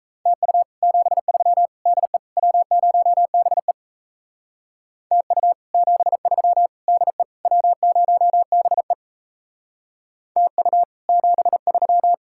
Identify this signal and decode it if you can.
Morse code, heard as a keyed tone, about 33 wpm, 700 Hz. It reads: TU 73 DE W0BE TU 73 DE W0BE TU 73
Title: Speed-Racing Common QSO Phrases Encore - 28wpm